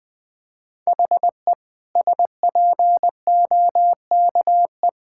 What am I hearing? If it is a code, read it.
Morse code, heard as a keyed tone, about 20 wpm, 700 Hz. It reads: HE SPOKE
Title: Sentences with 2-Word Phrases - Spoken Beginning - 20wpm